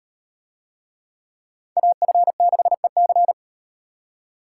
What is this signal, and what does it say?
Morse code, heard as a keyed tone, about 38 wpm, 700 Hz. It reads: AF6EC